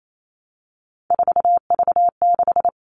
Morse code, heard as a keyed tone, about 28 wpm, 700 Hz. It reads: 4V6
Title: All Cut Numbers - 3-Digit 28wpm